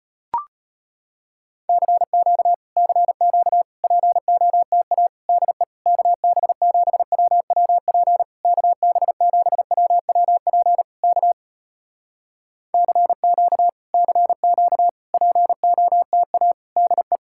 Morse code, heard as a keyed tone, about 34 wpm, 700 Hz. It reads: CQ CQ POTA DE KB7WWP KB7WWP K CQ CQ POTA DE